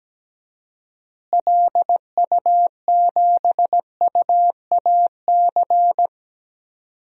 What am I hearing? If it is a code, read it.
Morse code, heard as a keyed tone, about 17 wpm, 700 Hz. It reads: LU7UAC
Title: DXCC-100 Call Signs - 6 Characters - Repeated 3x - 17wpm